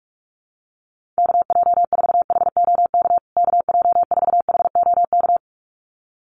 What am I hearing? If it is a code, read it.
Morse code, heard as a keyed tone, about 44 wpm, 700 Hz. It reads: KJ4HOK KJ4HOK
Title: USA Call Signs Repeated 2x 45wpm